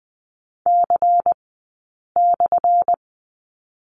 Morse code, heard as a keyed tone, about 20 wpm, spaced as for 8 wpm, 700 Hz. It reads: C/
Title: 177 Lesson 29d - Slash - Two Characters 20wpm